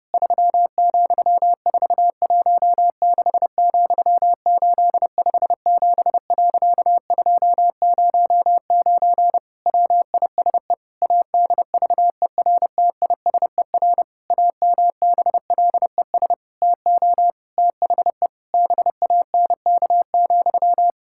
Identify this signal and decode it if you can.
Morse code, heard as a keyed tone, about 32 wpm, 700 Hz. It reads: 3,416,857.209 WISE ADVERTISER AMBLES TO THE BANK,
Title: Pangram Challenge v2 - 30wpm